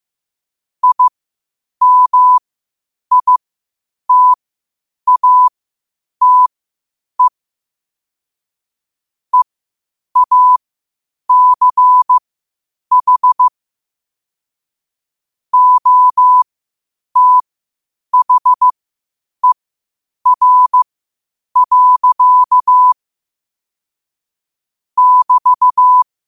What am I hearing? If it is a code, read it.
Morse code, heard as a keyed tone, about 15 wpm, spaced as for 8 wpm, 1000 Hz. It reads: IMITATE EACH OTHER. <BT>
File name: qotdfiles / 8wpm